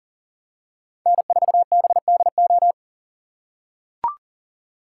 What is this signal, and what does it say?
Morse code, heard as a keyed tone, about 40 wpm, 690 Hz. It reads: N4BDO
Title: USA Call Signs - 3x Character Spacing - 40wpm